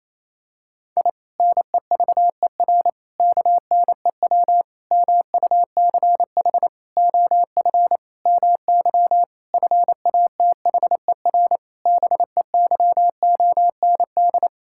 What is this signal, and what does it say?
Morse code, heard as a keyed tone, about 28 wpm, 700 Hz. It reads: I NEVER KNEW MUCH OF MY FATHER BEYOND
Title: Sentences from Top 1000 Words - Review Part 6 - 28wpm